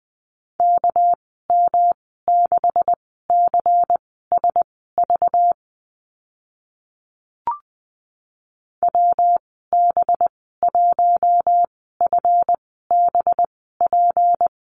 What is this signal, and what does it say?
Morse code, heard as a keyed tone, about 20 wpm, spaced as for 14 wpm, 700 Hz. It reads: KM6CSV WB1FBP
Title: 213 Lesson 34g - Number 8 - Call Signs 20wpm